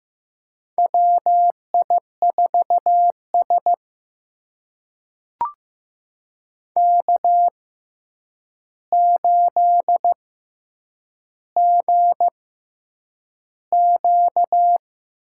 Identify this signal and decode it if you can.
Morse code, heard as a keyed tone, about 15 wpm, spaced as for 8 wpm, 700 Hz. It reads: WI4S K8GQ
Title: USA Call Signs - 4 Characters - 6x Character Spacing - 15wpm